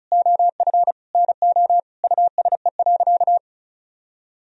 Morse code, heard as a keyed tone, about 34 wpm, 700 Hz. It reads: OF NO USE.
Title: Sentences from Top 1000 Words - Review Part 4 - 35wpm